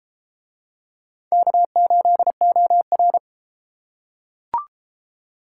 Morse code, heard as a keyed tone, about 32 wpm, 700 Hz. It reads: K8OR